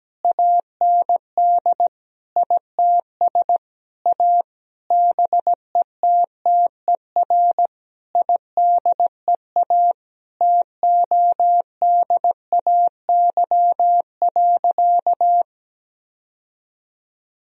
Morse code, heard as a keyed tone, about 17 wpm, 700 Hz. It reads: AND ITS A BETTER IDEA TODAY.
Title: Sentences from Top 1000 Words - Review Part 4 - 17wpm